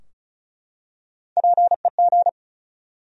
Morse code, heard as a keyed tone, about 36 wpm, 700 Hz. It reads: PEG
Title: USA Common Names - 35wpm